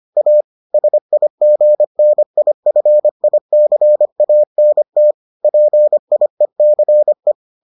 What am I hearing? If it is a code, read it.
Morse code, heard as a keyed tone, about 25 wpm, 600 Hz. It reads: A SIGNIFICANT PIECE